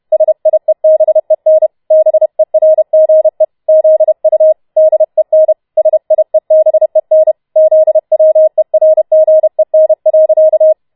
Morse code, heard as a keyed tone, about 31 wpm, 630 Hz. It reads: SIEBEN BERGE ZU DEN SIEBEN ZWERGEN.